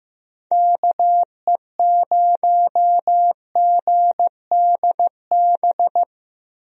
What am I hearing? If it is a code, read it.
Morse code, heard as a keyed tone, about 15 wpm, 700 Hz. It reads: KE0GDB
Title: USA Call Signs - 6 Characters - 15wpm